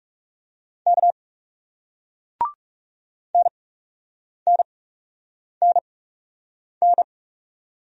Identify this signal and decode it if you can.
Morse code, heard as a keyed tone, about 38 wpm, spaced as for 8 wpm, 700 Hz. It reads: K NNNN